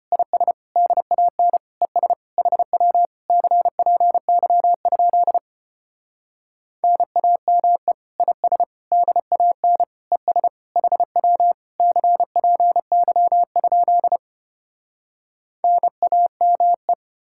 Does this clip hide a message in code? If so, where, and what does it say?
Yes: Morse code from 0.1 to 16.9 s, heard as a keyed tone, about 33 wpm, 700 Hz. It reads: IS DAN ES HW CPY? NAME IS DAN ES HW CPY? NAME